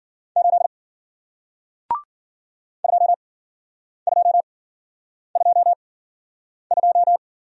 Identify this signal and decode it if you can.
Morse code, heard as a keyed tone, about 50 wpm, spaced as for 9 wpm, 700 Hz. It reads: 8 2222